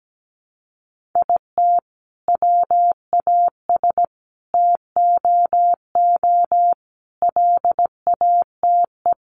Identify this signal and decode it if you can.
Morse code, heard as a keyed tone, about 17 wpm, 700 Hz. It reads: IT WAS TOO LATE
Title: Sentences from Top 600 Words 17wpm